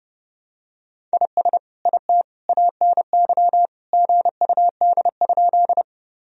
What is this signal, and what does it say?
Morse code, heard as a keyed tone, about 32 wpm, 700 Hz. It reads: IS IT ANY GUD?
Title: Sentences from Top 200 Words and Abbreviations 30wpm